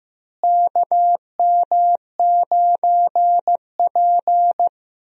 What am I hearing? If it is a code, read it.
Morse code, heard as a keyed tone, about 15 wpm, 700 Hz. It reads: KM9P
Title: USA Call Signs - 4 Characters - Repeated 2x - 15wpm